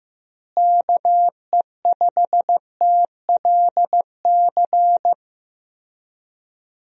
Morse code, heard as a keyed tone, about 15 wpm, 700 Hz. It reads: KE5TLC
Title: USA Call Signs - 6 Characters - Repeated 3x - 15wpm